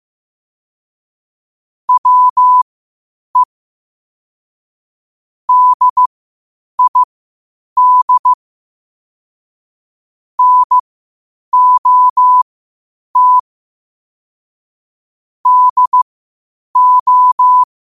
Morse code, heard as a keyed tone, about 15 wpm, spaced as for 8 wpm, 1000 Hz. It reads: WE DID NOT DO
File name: qotdfiles / 8wpm